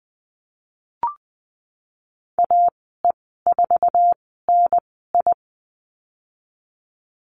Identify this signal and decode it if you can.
Morse code, heard as a keyed tone, about 20 wpm, spaced as for 15 wpm, 700 Hz. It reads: AE4NI
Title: 055 Lesson 10g - Letter r - Call Signs 20wpm